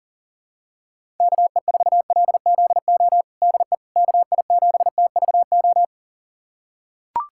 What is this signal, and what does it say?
Morse code, heard as a keyed tone, about 37 wpm, 700 Hz. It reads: KE4LZO DE KI7TVO
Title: USA Callsign DE Callsign - 6 Characters - 40wpm